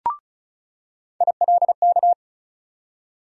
Morse code, heard as a keyed tone, about 34 wpm, 700 Hz. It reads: ILK
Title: USA Call Sign Postfixes - 3 Letters 35wpm